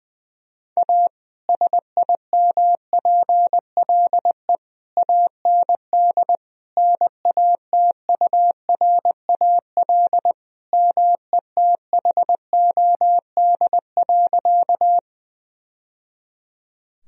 Morse code, heard as a keyed tone, about 20 wpm, 700 Hz. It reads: A SIMPLE AND NATURAL METHOD.
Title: Sentences from Top 1000 Words - Review Part 4 - 20wpm